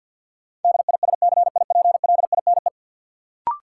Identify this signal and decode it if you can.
Morse code, heard as a keyed tone, about 53 wpm, 700 Hz. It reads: DISCIPLINE
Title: Sentences from Top 900 Words - New Word Drill - 50wpm